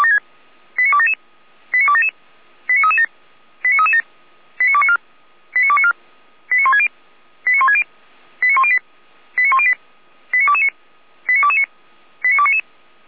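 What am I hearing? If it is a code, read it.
Five-tone selective call (ZVEI), heard as quick runs of five tones, tones of about 70 ms. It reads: 89277 89277 89337 89337 89294 89294 89160 89160 89108 89108 89229 89229 89299
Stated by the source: ZVEI five-tone selective call received by SDR.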